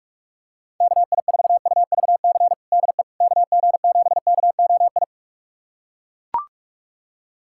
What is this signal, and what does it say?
Morse code, heard as a keyed tone, about 41 wpm, 700 Hz. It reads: KI4UVC DE KG7KOI
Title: USA Callsign DE Callsign - 6 Characters - 45wpm